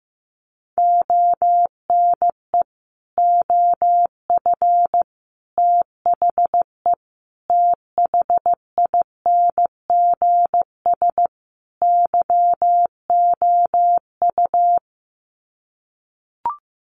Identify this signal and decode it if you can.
Morse code, heard as a keyed tone, about 15 wpm, 700 Hz. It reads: ONE OF THE THINGS YOU
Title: Sets of 5 Words Encore2 15wpm